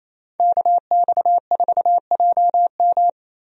Morse code, heard as a keyed tone, about 28 wpm, 700 Hz. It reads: KX4JM
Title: USA Call Signs - 5 Characters - Repeated 3x - 28wpm